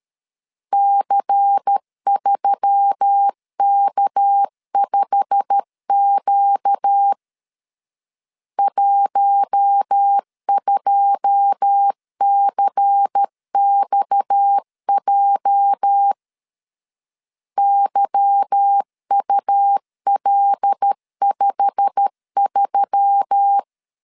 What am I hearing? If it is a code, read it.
Morse code, heard as a keyed tone, about 13 wpm, 800 Hz. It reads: C3K5Q 12CXJ YUL53
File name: LEZIONI / 9 2 12wpm 2301